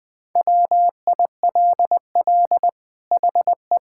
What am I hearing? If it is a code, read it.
Morse code, heard as a keyed tone, about 21 wpm, 700 Hz. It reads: WILL HE
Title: Sentences from Top 1000 Words - Review Part 6 - 20wpm